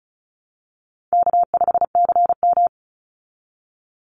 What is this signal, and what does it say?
Morse code, heard as a keyed tone, about 34 wpm, 700 Hz. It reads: K5CM
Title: Call Signs Repeated 2x - 35wpm